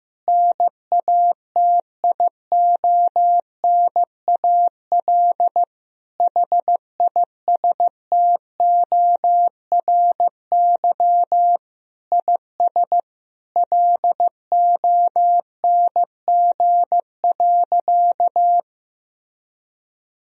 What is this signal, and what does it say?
Morse code, heard as a keyed tone, about 16 wpm, 700 Hz. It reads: NATIONAL HISTORY IS LONG.
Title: Sentences from Top 1000 Words - Review Part 4 - 15wpm